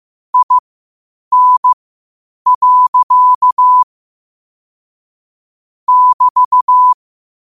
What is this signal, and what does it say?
Morse code, heard as a keyed tone, about 15 wpm, spaced as for 8 wpm, 1000 Hz. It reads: IN. <BT>
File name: qotdfiles / 8wpm